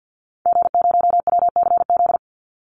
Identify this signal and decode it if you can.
Morse code, heard as a keyed tone, about 50 wpm, 700 Hz. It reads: G0WCZ
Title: Call Signs - 50wpm